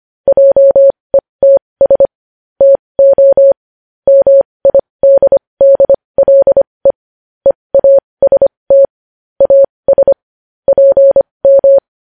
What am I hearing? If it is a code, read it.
Morse code, heard as a keyed tone, about 25 wpm, spaced as for 20 wpm, 550 Hz. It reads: JETS TO MIDDLE EAST AS PM
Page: News headlines in Morse code at 20 WPM